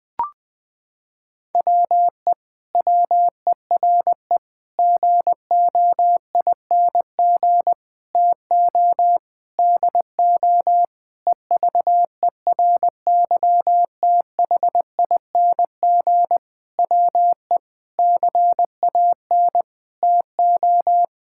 Morse code, heard as a keyed tone, about 21 wpm, 700 Hz. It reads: WE WERE GOING TO DO EVERYTHING WE CAN TO
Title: Sentences with 8-Word Phrases - 20wpm